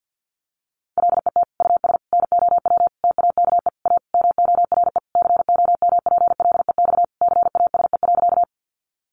Morse code, heard as a keyed tone, about 50 wpm, 700 Hz. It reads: LET US NOW TAKE A MORE COMPLEX CASE.
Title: Sentences from Top 900 Words 50wpm